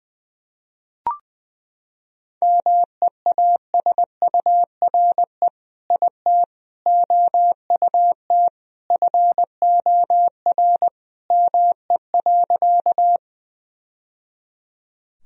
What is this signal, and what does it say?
Morse code, heard as a keyed tone, about 20 wpm, 700 Hz. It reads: MEASURE IT OUT FOR ME.
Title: Sentences from Top 1000 Words - Review Part 2 - 20wpm